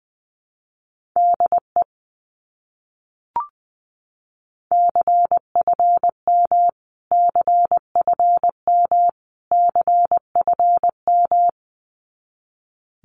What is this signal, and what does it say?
Morse code, heard as a keyed tone, about 19 wpm, 700 Hz. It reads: DE CFM CFM CFM
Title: Abbreviations Repeated 3x - 20wpm